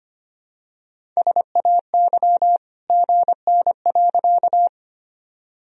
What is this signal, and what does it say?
Morse code, heard as a keyed tone, about 25 wpm, 700 Hz. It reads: SAY GN.
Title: Sentences from Top 300 Words and Abbreviations 25wpm